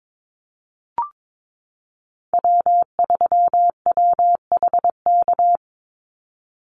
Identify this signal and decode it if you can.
Morse code, heard as a keyed tone, about 22 wpm, 700 Hz. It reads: W3WHK
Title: Call Signs - 22wpm